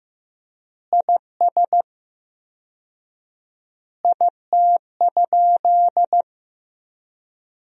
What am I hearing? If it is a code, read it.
Morse code, heard as a keyed tone, about 15 wpm, 700 Hz. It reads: IS IT?
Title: 4x Word-Spacing - Sentences from Top 200 Words - 15wpm